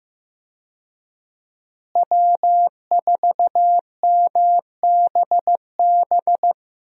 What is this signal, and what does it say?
Morse code, heard as a keyed tone, about 15 wpm, 700 Hz. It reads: W4MBB
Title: USA Call Signs - 5 Characters - Repeated 2x - 15wpm